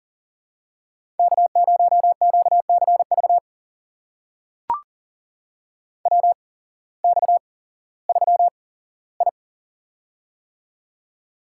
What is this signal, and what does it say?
Morse code, heard as a keyed tone, about 40 wpm, spaced as for 19 wpm, 700 Hz. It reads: K0QCV WX3I
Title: USA Call Signs - 8x Character Spacing - 40wpm